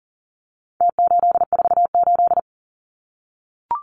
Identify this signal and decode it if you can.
Morse code, heard as a keyed tone, about 40 wpm, 700 Hz. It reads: T848
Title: Numbers - Contest Style - 4-Digit 40wpm